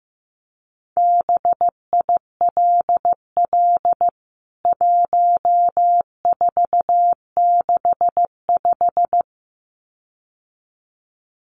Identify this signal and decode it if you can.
Morse code, heard as a keyed tone, about 15 wpm, 700 Hz. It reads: BILL 1465
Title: CWT Exchange 15wpm